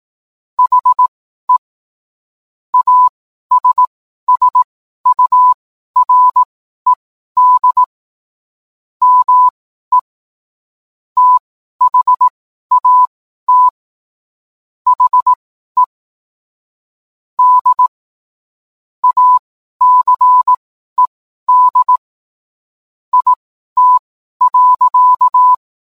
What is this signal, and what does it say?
Morse code, heard as a keyed tone, about 18 wpm, spaced as for 12 wpm, 1000 Hz. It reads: HE ASSURED ME THAT HE D ACED IT.